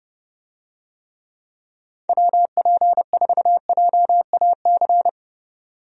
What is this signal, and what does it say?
Morse code, heard as a keyed tone, about 30 wpm, 700 Hz. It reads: WP4JAC
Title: USA Call Signs - 6 Characters - 30wpm